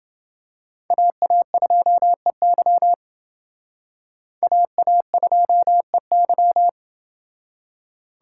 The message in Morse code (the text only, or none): AA2EY AA2EY